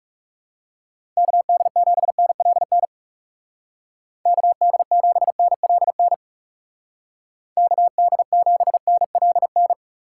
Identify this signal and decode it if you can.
Morse code, heard as a keyed tone, about 38 wpm, 700 Hz. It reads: KD7NLN KD7NLN KD7NLN
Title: Speed Racing - USA 2x3 Call Signs 30wpm